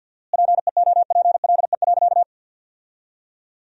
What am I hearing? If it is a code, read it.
Morse code, heard as a keyed tone, about 47 wpm, 700 Hz. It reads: PEOPLE.
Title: 4x Word-Spacing - Sentences from Top 500 Words - 50wpm